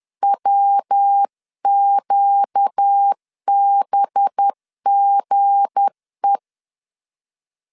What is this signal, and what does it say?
Morse code, heard as a keyed tone, about 10 wpm, 800 Hz. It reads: WQBGE